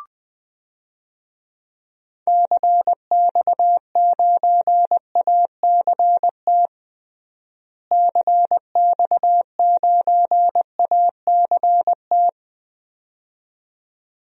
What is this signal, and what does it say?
Morse code, heard as a keyed tone, about 20 wpm, 700 Hz. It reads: CX9ACT CX9ACT
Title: DXCC-100 Call Signs - 6 Characters - Repeated 2x - 20wpm